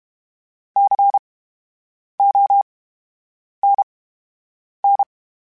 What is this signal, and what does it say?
Morse code, heard as a keyed tone, about 32 wpm, spaced as for 8 wpm, 800 Hz. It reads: CONN
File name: Special INGs / Special INGs-mp3s / VerySlowToFaster Ing @30